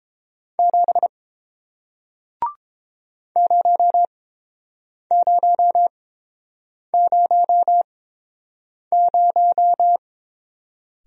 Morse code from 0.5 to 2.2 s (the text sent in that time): 7